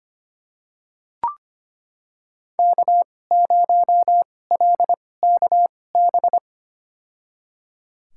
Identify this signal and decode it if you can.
Morse code, heard as a keyed tone, about 25 wpm, spaced as for 18 wpm, 700 Hz. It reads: K0LKB
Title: USA Call Signs - 2x Character Spacing - 25wpm